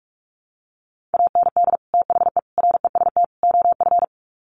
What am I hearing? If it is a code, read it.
Morse code, heard as a keyed tone, about 44 wpm, 700 Hz. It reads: AND THE REST OF
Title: Sets of 4 Words 45wpm